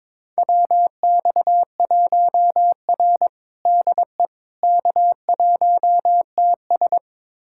Morse code, heard as a keyed tone, about 23 wpm, 700 Hz. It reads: WX1R DE K1TS
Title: USA Callsign DE Callsign - 4 Characters - 22wpm